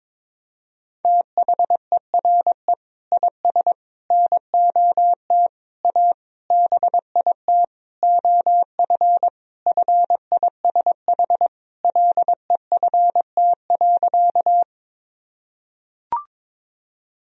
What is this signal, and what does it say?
Morse code, heard as a keyed tone, about 22 wpm, 700 Hz. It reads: THERE IS NOT A BIT OF FISH LEFT.
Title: Sentences from Top 1000 Words - Review Part 1 - 22wpm